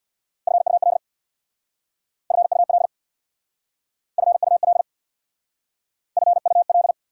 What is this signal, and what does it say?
Morse code, heard as a keyed tone, about 66 wpm, 690 Hz. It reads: FUL FUL FUL FUL